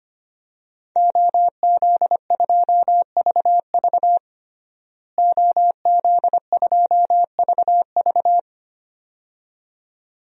Morse code, heard as a keyed tone, about 25 wpm, 700 Hz. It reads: OZ2VV OZ2VV
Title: DXCC-100 Call Signs - 5 Characters - Repeated 2x - 25wpm